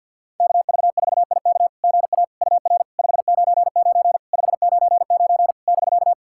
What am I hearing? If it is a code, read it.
Morse code, heard as a keyed tone, about 49 wpm, 700 Hz. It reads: KV4IK GA UR 599 599 <BK>
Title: SOTA QSO Practice 50wpm